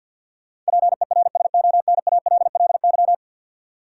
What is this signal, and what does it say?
Morse code, heard as a keyed tone, about 53 wpm, 700 Hz. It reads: PERSONALLY
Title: Sentences from Top 1400 Words - New Word Drill - 50wpm